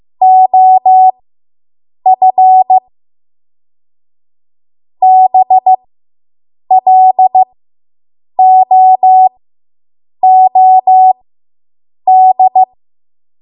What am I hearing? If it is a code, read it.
Morse code, heard as a keyed tone, about 15 wpm, spaced as for 7 wpm, 750 Hz. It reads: OF BLOOD